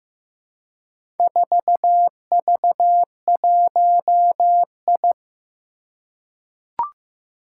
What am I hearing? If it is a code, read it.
Morse code, heard as a keyed tone, about 15 wpm, 700 Hz. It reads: 4V1I
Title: Four Challenging Letter-Number Combinations 15wpm